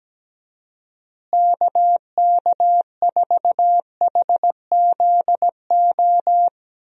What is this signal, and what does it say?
Morse code, heard as a keyed tone, about 17 wpm, 700 Hz. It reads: KK4HZO
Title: USA Call Signs - 6 Characters - Repeated 2x - 17wpm